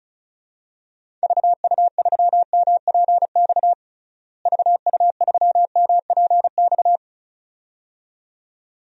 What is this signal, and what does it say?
Morse code, heard as a keyed tone, about 34 wpm, 700 Hz. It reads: VU3MPX VU3MPX
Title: DXCC-100 Call Signs - 6 Characters - Repeated 2x - 35wpm